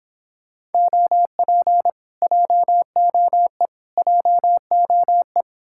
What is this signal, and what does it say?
Morse code, heard as a keyed tone, about 29 wpm, 700 Hz. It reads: OP JOE JOE